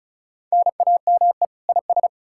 Morse code, heard as a keyed tone, about 47 wpm, spaced as for 41 wpm, 700 Hz. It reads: NAME IS
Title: Speed-Racing Common QSO Phrases - 35wpm